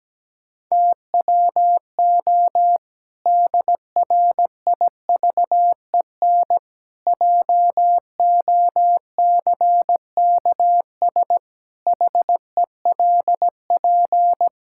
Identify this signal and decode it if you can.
Morse code, heard as a keyed tone, about 17 wpm, 700 Hz. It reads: TWO DRIVEN JOCKS HELP
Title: Pangram Challenge v2 - 17wpm